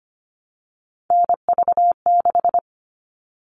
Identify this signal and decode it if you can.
Morse code, heard as a keyed tone, about 25 wpm, 700 Hz. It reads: NV6
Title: All Cut Numbers - 3-Digit 25wpm